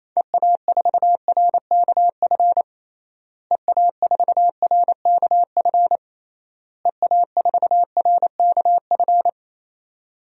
Morse code, heard as a keyed tone, about 27 wpm, 700 Hz. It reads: EA4RKF EA4RKF EA4RKF
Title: DXCC-100 Call Signs - 6 Characters - Repeated 3x - 28wpm